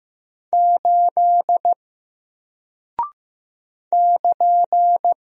Morse code, heard as a keyed tone, about 15 wpm, 700 Hz. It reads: 8 <KN>
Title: Single Character (Letter:Number:Punctuation:Prosign) Alt 15wpm